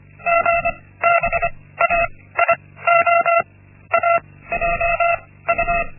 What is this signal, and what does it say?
Morse code, heard as a keyed tone, about 25 wpm, 680 Hz. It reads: G B A I O A J U
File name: Linksmakalnis-1